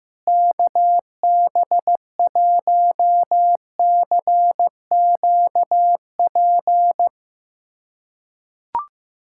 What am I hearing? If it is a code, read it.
Morse code, heard as a keyed tone, about 15 wpm, 700 Hz. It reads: KB1CQP
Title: USA Callsign DE Callsign 15wpm